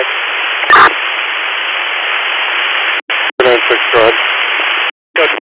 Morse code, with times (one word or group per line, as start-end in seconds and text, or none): none